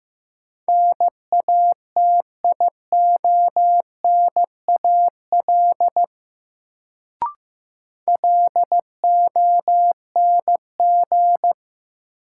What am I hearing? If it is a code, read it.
Morse code, heard as a keyed tone, about 15 wpm, 700 Hz. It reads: NATIONAL LONG
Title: Sets of 2 Words - New Word Drill - 15wpm